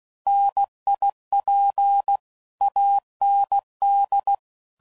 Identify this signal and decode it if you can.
Morse code, heard as a keyed tone, about 16 wpm, 800 Hz. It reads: NIP AND